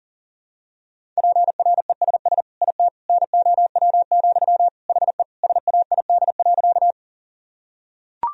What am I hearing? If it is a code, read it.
Morse code, heard as a keyed tone, about 40 wpm, 700 Hz. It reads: PRESS IT NOW, HE SAID.
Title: Sentences from Top 1000 Words - Review Part 1 - 40wpm